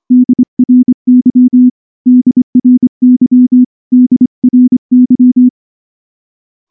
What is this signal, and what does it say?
Morse code, heard as a keyed tone, about 23 wpm, 250 Hz. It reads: DRY DRY DRY
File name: morse / morse-training-25WPM